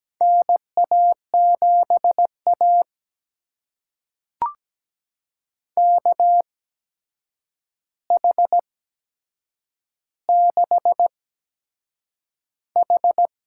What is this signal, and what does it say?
Morse code, heard as a keyed tone, about 17 wpm, spaced as for 7 wpm, 700 Hz. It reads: NA7A KH6H
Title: USA Call Signs - 4 Characters - 8x Character Spacing - 17wpm